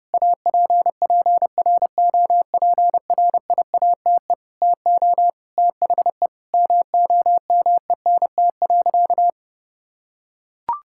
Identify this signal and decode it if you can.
Morse code, heard as a keyed tone, about 28 wpm, 700 Hz. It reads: APPROPRIATE TO THE MOMENT.
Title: Sentences from Top 1000 Words - Review Part 5 - 30wpm